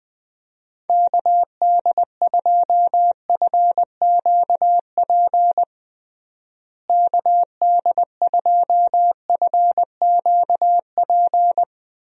Morse code, heard as a keyed tone, about 21 wpm, 700 Hz. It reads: KD2FQP KD2FQP
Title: USA Call Signs - 6 Characters - Repeated 3x - 20wpm